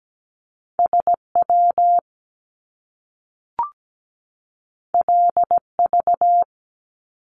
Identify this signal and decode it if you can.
Morse code, heard as a keyed tone, about 17 wpm, 700 Hz. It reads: SW LV
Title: Top Two Letter Combinations in Words 17wpm